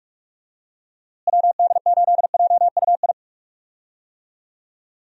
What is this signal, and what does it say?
Morse code, heard as a keyed tone, about 44 wpm, 700 Hz. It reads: WD8JUI
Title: USA Call Signs - 6 Characters - Repeated 3x - 45wpm